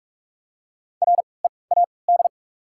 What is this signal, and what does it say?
Morse code, heard as a keyed tone, about 47 wpm, spaced as for 26 wpm, 700 Hz. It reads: READ
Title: 3x Character Spacing - Top 200 Words - 45wpm